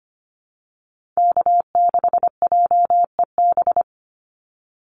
Morse code, heard as a keyed tone, about 25 wpm, 700 Hz. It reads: K6JEB
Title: Call Signs Repeated 3x - 25wpm